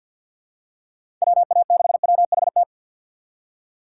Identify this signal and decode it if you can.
Morse code, heard as a keyed tone, about 47 wpm, 700 Hz. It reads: WA6WHT
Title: USA Call Signs - 2x Character Spacing - 50wpm